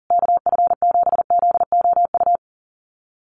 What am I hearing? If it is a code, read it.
Morse code, heard as a keyed tone, about 37 wpm, 700 Hz. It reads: KF7ZOU
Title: USA Call Signs Repeated 3x 40wpm